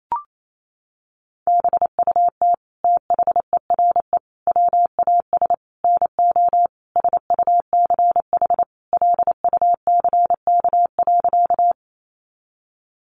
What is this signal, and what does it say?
Morse code, heard as a keyed tone, about 28 wpm, 700 Hz. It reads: BUT THERE WAS NO SUCH LUCK.
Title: Sentences from Top 1500 Words 28wpm